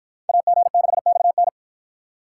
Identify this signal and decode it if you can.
Morse code, heard as a keyed tone, about 53 wpm, 700 Hz. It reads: AG6XN